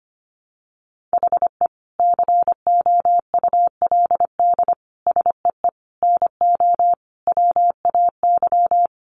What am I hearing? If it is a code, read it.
Morse code, heard as a keyed tone, about 25 wpm, 700 Hz. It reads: HE COULD SEE NO WAY
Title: Sentences from Top 1200 Words 25wpm